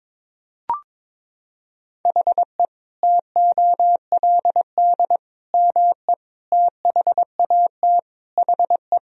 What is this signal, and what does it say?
Morse code, heard as a keyed tone, about 22 wpm, 700 Hz. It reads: HE TOLD ME THAT HE
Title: Sentences from Top 1000 Words - Review Part 3 - 22wpm